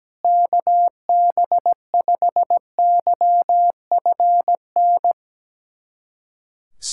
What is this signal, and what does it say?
Morse code, heard as a keyed tone, about 17 wpm, 690 Hz. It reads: KB5YFN
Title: SOTA QSO Practice 17wpm